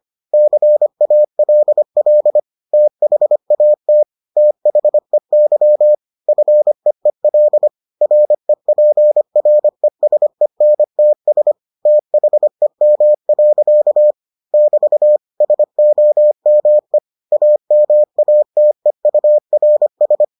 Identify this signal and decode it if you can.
Morse code, heard as a keyed tone, about 25 wpm, 600 Hz. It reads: CALL THAT THEY FEEL REPRESENTS THEM. <BT> SOME AMATEURS